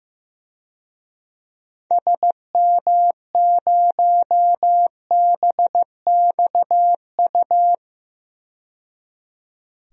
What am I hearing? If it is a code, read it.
Morse code, heard as a keyed tone, about 15 wpm, 700 Hz. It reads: SM0BXU
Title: DXCC-100 Call Signs - 6 Characters - 15wpm